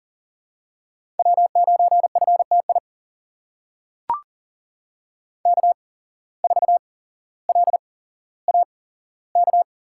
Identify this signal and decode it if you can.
Morse code, heard as a keyed tone, about 39 wpm, spaced as for 17 wpm, 700 Hz. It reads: W9FTI K4LAK
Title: USA Call Signs - 5 Characters - 8x Character Spacing - 40wpm